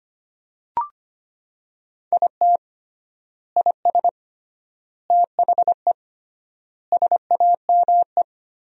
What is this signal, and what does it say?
Morse code, heard as a keyed tone, about 25 wpm, 700 Hz. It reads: IT IS THE SAME
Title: 3x Word-Spacing - Sentences from Top 200 Words - 25wpm